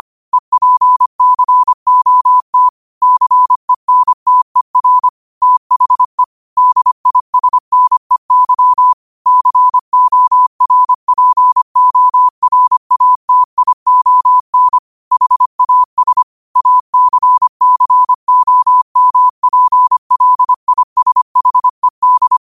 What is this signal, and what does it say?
Morse code, heard as a keyed tone, about 25 wpm, 1000 Hz. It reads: EPCOT CENTER THE DISNEY CORPORATION HAS ACCOMPLISHED
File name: qotdfiles / 25wpm